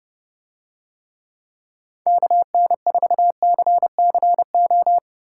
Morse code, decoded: KN4CCO

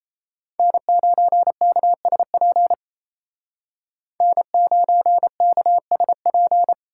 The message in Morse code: N9KSP N9KSP

